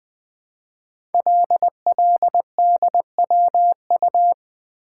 Morse code, heard as a keyed tone, about 20 wpm, 700 Hz. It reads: LLDWU
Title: Code Groups - 5 Letter:Numbers - 20wpm